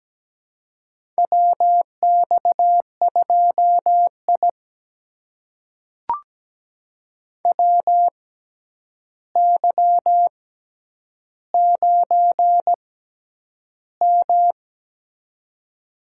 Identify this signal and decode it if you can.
Morse code, heard as a keyed tone, about 17 wpm, spaced as for 9 wpm, 700 Hz. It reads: WX2I WY9M
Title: USA Call Signs - 4 Characters - 6x Character Spacing - 17wpm